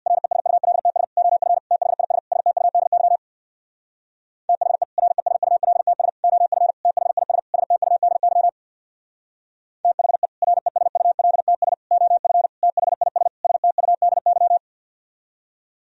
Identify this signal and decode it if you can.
Morse code, heard as a keyed tone, about 62 wpm, 700 Hz. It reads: RESULTS OF THIS STUDY THE RESULTS OF THIS STUDY THE RESULTS OF THIS STUDY
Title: Speed-Racing - Sets of 5 Words 50wpm